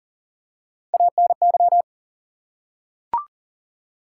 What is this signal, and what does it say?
Morse code, heard as a keyed tone, about 42 wpm, 700 Hz. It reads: ANY
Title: Three Letter Words - 40wpm